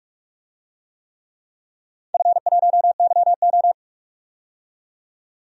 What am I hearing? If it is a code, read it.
Morse code, heard as a keyed tone, about 46 wpm, 700 Hz. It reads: F1YO
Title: DXCC-100 Call Signs - 4 Characters - 45wpm